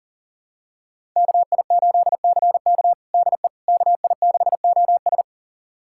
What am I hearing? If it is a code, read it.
Morse code, heard as a keyed tone, about 40 wpm, 700 Hz. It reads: KI8CK DE KI6OS
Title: USA Callsign DE Callsign - 5 Characters - 40wpm